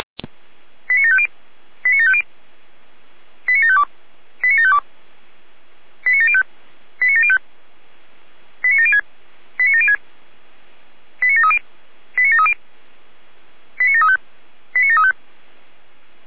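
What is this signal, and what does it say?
Five-tone selective call (ZVEI), heard as quick runs of five tones, tones of about 70 ms. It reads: 89750 89750 89752 89752 89705 89705 89706 89706 89730 89730 89735 89735
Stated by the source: ZVEI five-tone selective call received with an SDR.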